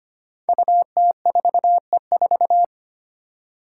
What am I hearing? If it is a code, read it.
Morse code, heard as a keyed tone, about 25 wpm, 700 Hz. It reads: UT4E4
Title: Five Challenging Letter-Number Combinations 25wpm